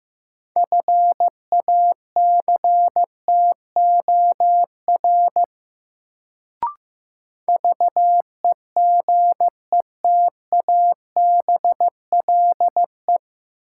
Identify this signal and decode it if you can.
Morse code, heard as a keyed tone, about 15 wpm, 700 Hz. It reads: FACTOR VEGETABLE
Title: Sentences from Top 1300 Words - New Word Drill Alt - 15wpm